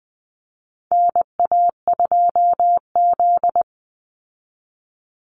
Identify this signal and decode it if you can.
Morse code, heard as a keyed tone, about 20 wpm, 700 Hz. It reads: NA2Z
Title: USA Call Signs 20wpm